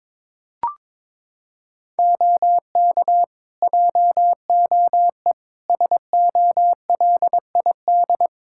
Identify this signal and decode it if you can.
Morse code, heard as a keyed tone, about 22 wpm, 700 Hz. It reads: OK JOE SOLID
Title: K1USN SST QSO Practice 22wpm